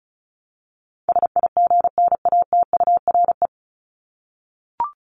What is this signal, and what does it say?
Morse code, heard as a keyed tone, about 34 wpm, 700 Hz. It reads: SIGNATURE
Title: Top 1751-2000 Words 35wpm